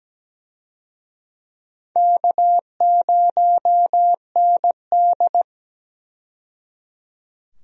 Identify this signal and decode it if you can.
Morse code, heard as a keyed tone, about 17 wpm, 700 Hz. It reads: K0ND
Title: USA Call Signs - 4 Characters - 17wpm